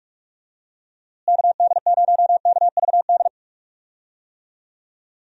Morse code, heard as a keyed tone, about 41 wpm, 700 Hz. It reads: KD0KVD
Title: USA Call Signs - 6 Characters - Repeated 2x - 45wpm